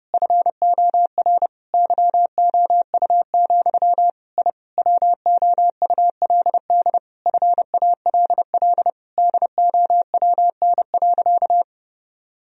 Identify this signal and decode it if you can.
Morse code, heard as a keyed tone, about 30 wpm, 700 Hz. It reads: FOR YOU, I WOULD FALL DOWN.
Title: Sentences from Top 1000 Words - Review Part 2 - 30wpm